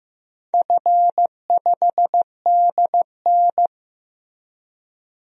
Morse code, heard as a keyed tone, about 15 wpm, 700 Hz. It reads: F5DN
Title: DXCC-100 Call Signs - 4 Characters - Repeated 2x - 15wpm